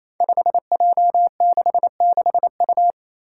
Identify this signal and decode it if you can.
Morse code, heard as a keyed tone, about 27 wpm, 700 Hz. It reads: 5J66U
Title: Five Challenging Letter-Number Combinations 28wpm